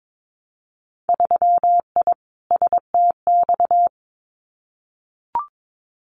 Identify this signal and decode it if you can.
Morse code, heard as a keyed tone, about 22 wpm, 700 Hz. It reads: 3I STX
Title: Winter Field Day Exchanges Encore 22wpm